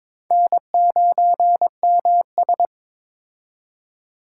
Morse code, heard as a keyed tone, about 22 wpm, 700 Hz. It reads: N9MS
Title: K1USN SST QSO Practice - 22wpm